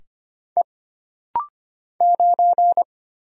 Morse code, heard as a keyed tone, about 25 wpm, 700 Hz. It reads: E 9
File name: Single Character (Letter:Number:Punctuation:Prosign) - Rapid-Fire - 25wpm